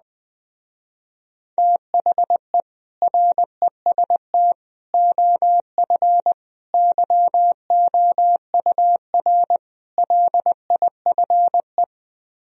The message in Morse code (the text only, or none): THE REST OF YOUR LIFE